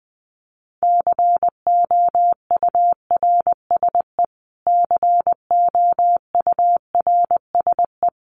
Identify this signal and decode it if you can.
Morse code, heard as a keyed tone, about 20 wpm, 700 Hz. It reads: COURSE COURSE
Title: Top 500 Words Repeated 20wpm